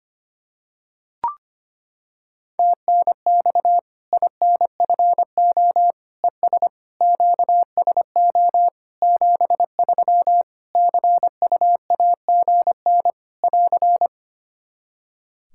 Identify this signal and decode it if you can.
Morse code, heard as a keyed tone, about 25 wpm, 700 Hz. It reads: TNX INFO ES QSO 73 CUAGN <AR>
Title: Practice QSO - Exchange 4 - Ending - 25wpm